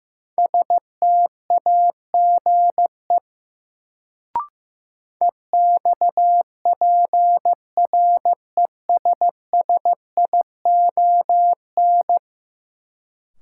Morse code, heard as a keyed tone, about 15 wpm, 700 Hz. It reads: STAGE EXPRESSION
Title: Sentences from Top 1200 Words - New Word Drill Alt - 15wpm